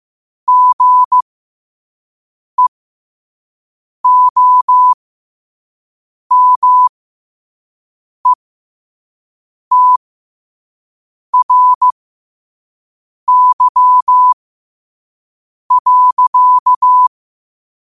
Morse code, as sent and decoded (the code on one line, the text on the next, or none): --. --- -- . - .-. -.-- .-.-.-
GEOMETRY.